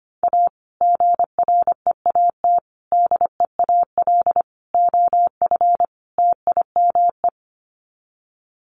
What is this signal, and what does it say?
Morse code, heard as a keyed tone, about 25 wpm, 700 Hz. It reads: A GREAT DEAL OF TIME
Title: Sets of 5 Words Encore2 25wpm